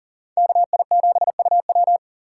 Morse code, heard as a keyed tone, about 40 wpm, 700 Hz. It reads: KI7UW
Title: USA Call Signs - 5 Characters - 4x Character Spacing - 40wpm